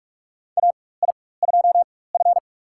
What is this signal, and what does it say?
Morse code, heard as a keyed tone, about 42 wpm, spaced as for 21 wpm, 700 Hz. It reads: AI2F
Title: USA Call Signs - 4 Characters - 4x Character Spacing - 45wpm